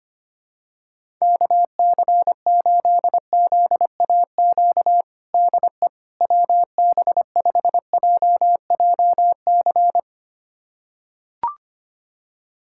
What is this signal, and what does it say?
Morse code, heard as a keyed tone, about 25 wpm, 700 Hz. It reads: KC8ZAQ DE WB5JJC
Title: USA Callsign DE Callsign 25wpm